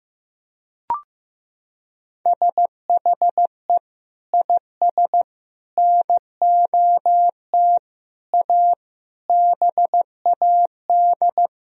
Morse code, heard as a keyed tone, about 16 wpm, 700 Hz. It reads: SHE IS NOT A BAD
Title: Sentences from Top 1000 Words - Review Part 3 - 15wpm